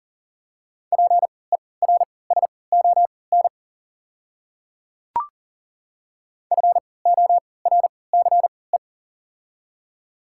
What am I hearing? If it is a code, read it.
Morse code, heard as a keyed tone, about 39 wpm, spaced as for 22 wpm, 700 Hz. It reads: PERSON FORCE
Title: 3x Character Spacing - Top 500 Words - 40wpm